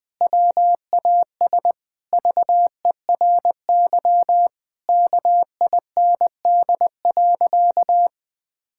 Morse code, heard as a keyed tone, about 20 wpm, 700 Hz. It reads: WAS VERY KIND.
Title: Sentences from Top 1600 Words 20wpm